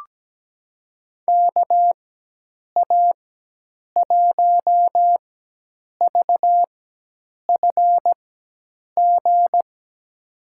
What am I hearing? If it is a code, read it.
Morse code, heard as a keyed tone, about 17 wpm, spaced as for 8 wpm, 700 Hz. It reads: KA1VFG
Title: USA Call Signs - 6 Characters - 4x Character Spacing - 17wpm